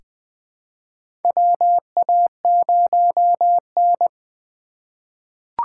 Morse code, heard as a keyed tone, about 20 wpm, 700 Hz. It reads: WA0N
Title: USA Call Signs - 4x Character Spacing - 20wpm